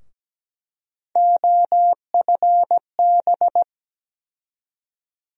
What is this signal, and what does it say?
Morse code, heard as a keyed tone, about 17 wpm, 700 Hz. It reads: OFB
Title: USA Call Sign Postfixes - 3 Letters 17wpm